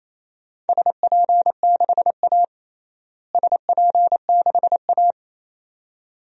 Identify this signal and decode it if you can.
Morse code, heard as a keyed tone, about 28 wpm, 700 Hz. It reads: SP6A SP6A
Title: DXCC-100 Call Signs - 4 Characters - Repeated 2x - 28wpm